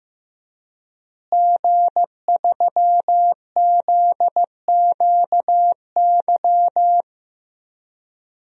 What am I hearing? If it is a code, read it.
Morse code, heard as a keyed tone, about 15 wpm, 700 Hz. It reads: G3ZQY
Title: DXCC-100 Call Signs - 5 Characters - Repeated 3x - 15wpm